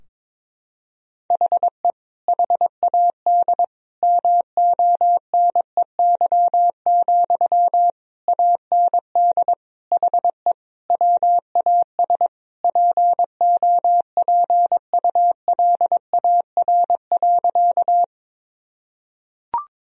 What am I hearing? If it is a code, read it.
Morse code, heard as a keyed tone, about 22 wpm, 700 Hz. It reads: HE HAD MONEY, AND HE WAS POPULAR.
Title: Sentences from Top 1000 Words - Review Part 3 - 22wpm